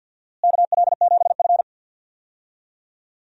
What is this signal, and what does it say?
Morse code, heard as a keyed tone, about 51 wpm, 700 Hz. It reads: KL7F
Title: DXCC-100 Call Signs - 4 Characters - Repeated 3x - 50wpm